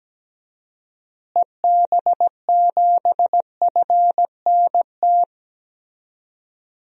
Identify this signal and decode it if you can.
Morse code, heard as a keyed tone, about 17 wpm, 700 Hz. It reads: EB7FNT
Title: DXCC-100 Call Signs - 6 Characters - Repeated 2x - 17wpm